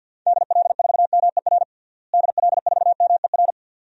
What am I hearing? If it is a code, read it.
Morse code, heard as a keyed tone, about 49 wpm, 700 Hz. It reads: DL4MER DL4MER
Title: DXCC-100 Call Signs - 6 Characters - Repeated 2x - 50wpm